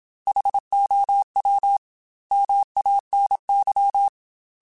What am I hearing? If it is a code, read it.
Morse code, heard as a keyed tone, about 27 wpm, 800 Hz. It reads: HOW MANY
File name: Technical Subjects / Technical Subjects-mp3s / OR Gate @25